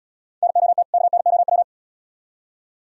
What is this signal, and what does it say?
Morse code, heard as a keyed tone, about 73 wpm, 700 Hz. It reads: NOT KNOW